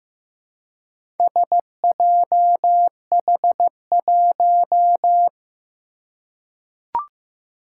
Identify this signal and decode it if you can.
Morse code, heard as a keyed tone, about 15 wpm, 700 Hz. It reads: SJH1
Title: Four Challenging Letter-Number Combinations 15wpm